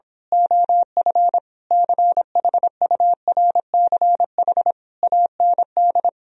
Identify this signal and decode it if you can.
Morse code, heard as a keyed tone, about 26 wpm, 700 Hz. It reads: OF CHURCH AND
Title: Speed-Racing - Sets of 5 Words 22wpm